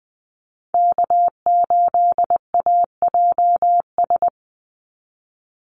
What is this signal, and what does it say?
Morse code, heard as a keyed tone, about 20 wpm, 700 Hz. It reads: K8AJS
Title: Call Signs - 20wpm